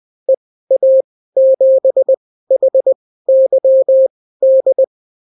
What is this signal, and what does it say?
Morse code, heard as a keyed tone, about 20 wpm, spaced as for 14 wpm, 530 Hz. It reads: EA7HYD
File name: CW QSO libroCW4 20-10ppm